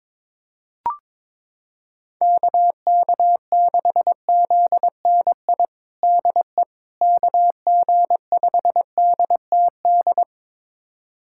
Morse code, heard as a keyed tone, about 21 wpm, 700 Hz. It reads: KK6ZNI DE KG5DTD
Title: USA Callsign DE Callsign - 6 Characters - 22wpm